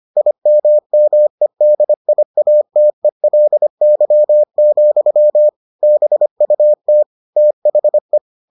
Morse code, heard as a keyed tone, about 25 wpm, 600 Hz. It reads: IMMEDIATELY, BUT THE